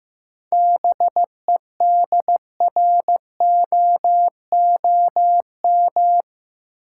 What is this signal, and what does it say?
Morse code, heard as a keyed tone, about 15 wpm, 700 Hz. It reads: BEDROOM
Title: Seven Letter Words - 15wpm